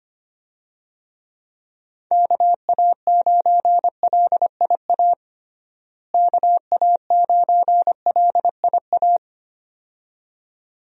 Morse code, heard as a keyed tone, about 25 wpm, 700 Hz. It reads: KA9LIA KA9LIA
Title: USA Call Signs - 6 Characters - Repeated 2x - 25wpm